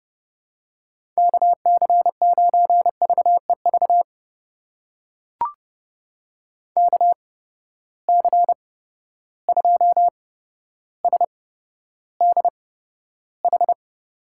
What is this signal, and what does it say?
Morse code, heard as a keyed tone, about 30 wpm, spaced as for 13 wpm, 700 Hz. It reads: KC9VEV KC2SDH
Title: USA Call Signs - 6 Characters - 8x Character Spacing - 30wpm